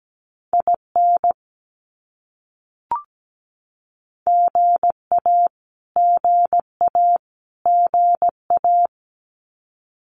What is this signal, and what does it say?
Morse code, heard as a keyed tone, about 17 wpm, 700 Hz. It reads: IN GA GA GA
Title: US States Repeated 3x 17wpm